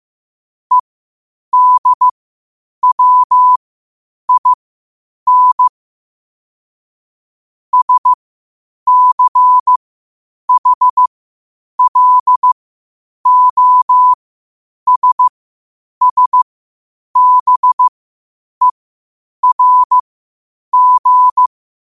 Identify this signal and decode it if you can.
Morse code, heard as a keyed tone, about 15 wpm, spaced as for 8 wpm, 1000 Hz. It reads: EDWIN SCHLOSSBERG
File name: qotdfiles / 8wpm